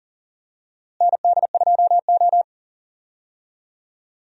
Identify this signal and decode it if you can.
Morse code, heard as a keyed tone, about 40 wpm, 700 Hz. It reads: ND2O
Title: USA Call Signs - 4 Characters - Repeated 2x - 40wpm